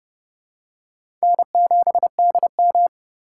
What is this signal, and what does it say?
Morse code, heard as a keyed tone, about 30 wpm, 700 Hz. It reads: N7DM
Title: USA Call Signs - 4 Characters - 3x Character Spacing - 30wpm